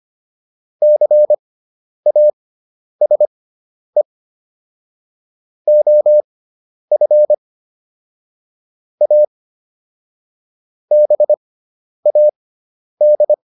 Morse code, heard as a keyed tone, about 25 wpm, spaced as for 10 wpm, 600 Hz. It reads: CASE OF A BAD